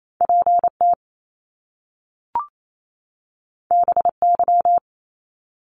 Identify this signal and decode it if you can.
Morse code, heard as a keyed tone, about 28 wpm, 700 Hz. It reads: PT BY
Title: Top Two Letter Combinations in Words 28wpm